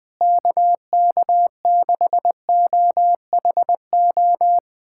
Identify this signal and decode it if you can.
Morse code, heard as a keyed tone, about 20 wpm, 700 Hz. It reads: KK6OHO
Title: USA Callsign DE Callsign 20wpm